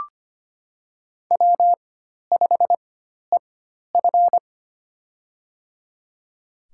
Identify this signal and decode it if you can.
Morse code, heard as a keyed tone, about 25 wpm, spaced as for 12 wpm, 700 Hz. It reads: W5EF
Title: USA Call Signs - 4x Character Spacing - 25wpm